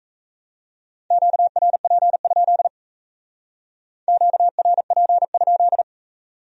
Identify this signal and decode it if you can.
Morse code, heard as a keyed tone, about 36 wpm, 700 Hz. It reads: QRP? QRP?